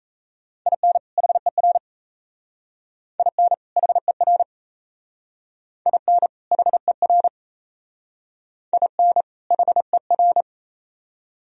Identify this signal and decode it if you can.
Morse code, heard as a keyed tone, about 35 wpm, 700 Hz. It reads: IN HER IN HER IN HER IN HER